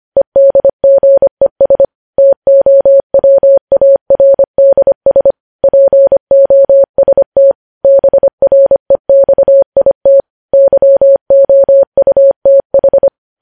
Morse code, heard as a keyed tone, about 25 wpm, 550 Hz. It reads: EDGES TOWARDS POST BREXIT YOUTH
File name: podcast / News headlines-1x-25x25